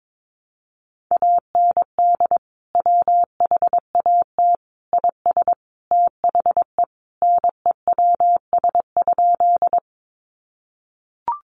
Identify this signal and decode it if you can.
Morse code, heard as a keyed tone, about 22 wpm, 700 Hz. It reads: AND WHAT IS THE NEWS?
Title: Sentences from Top 800 Words 22wpm